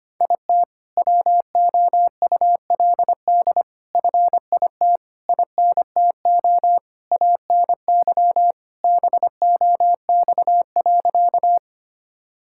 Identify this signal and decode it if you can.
Morse code, heard as a keyed tone, about 25 wpm, 700 Hz. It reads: IT WOULD FIT INTO ANY BOX.
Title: Sentences from Top 1000 Words - Review Part 5 - 25wpm